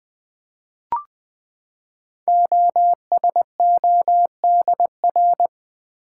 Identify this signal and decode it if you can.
Morse code, heard as a keyed tone, about 20 wpm, 700 Hz. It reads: OSODR
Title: Code Groups - 5 Letters - 20wpm